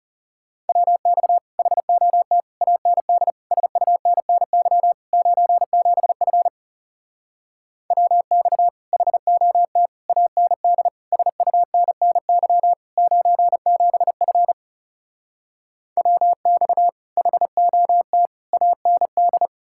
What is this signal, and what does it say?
Morse code, heard as a keyed tone, about 40 wpm, 700 Hz. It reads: WX HOT AND SUNNY 97F WX HOT AND SUNNY 97F WX HOT AND